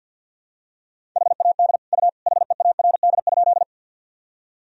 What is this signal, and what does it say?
Morse code, heard as a keyed tone, about 52 wpm, 700 Hz. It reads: HAD U HEARD?
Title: Sentences from Top 400 Words and Abbreviations 50wpm